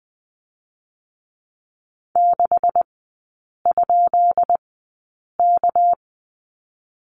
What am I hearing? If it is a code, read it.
Morse code, heard as a keyed tone, about 20 wpm, spaced as for 8 wpm, 700 Hz. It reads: 6?K
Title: 218 Lesson 35e - Number 0 - Three Characters 20wpm